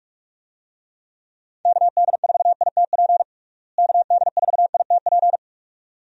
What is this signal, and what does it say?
Morse code, heard as a keyed tone, about 46 wpm, 700 Hz. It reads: KD4ITP KD4ITP
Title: USA Call Signs - 6 Characters - Repeated 2x - 45wpm